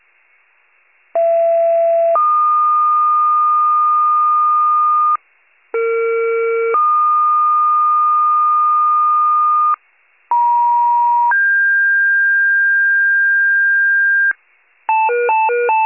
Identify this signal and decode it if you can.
Two-tone page, heard as a one-second tone, then a three-second tone, about 670 then 1220 Hz, about 460 then 1220 Hz, about 950 then 1600 Hz. It is heard from 1.1 to 14.3 s.
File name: wcalert